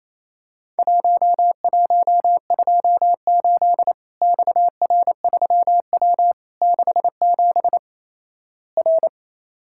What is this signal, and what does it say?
Morse code, heard as a keyed tone, about 28 wpm, 700 Hz. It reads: 1128 XR3W 67 R
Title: Worked All Europe DX Contest - QTC Exchange - 28wpm